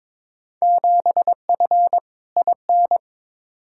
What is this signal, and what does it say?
Morse code, heard as a keyed tone, about 22 wpm, 700 Hz. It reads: 7F IN
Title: Field Day Exchanges Encore - 22wpm